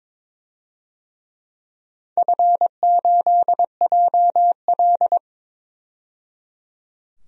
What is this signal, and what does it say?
Morse code, heard as a keyed tone, about 23 wpm, 700 Hz. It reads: F8JL
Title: DXCC-100 Call Signs - 4 Characters - 22wpm